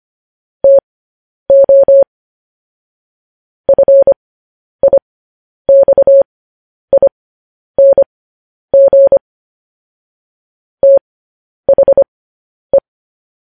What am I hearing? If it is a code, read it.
Morse code, heard as a keyed tone, about 25 wpm, spaced as for 10 wpm, 550 Hz. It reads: TO FIXING THE
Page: news headlines in morse code at 10 wpm